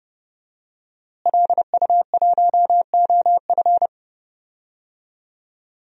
Morse code, heard as a keyed tone, about 31 wpm, 700 Hz. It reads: LU1OF
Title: DXCC-100 Call Signs - 5 Characters - Repeated 3x - 30wpm